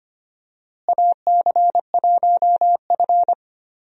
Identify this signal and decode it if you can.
Morse code, heard as a keyed tone, about 25 wpm, 700 Hz. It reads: AC1F
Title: USA Call Signs - 4 Characters - Repeated 2x - 25wpm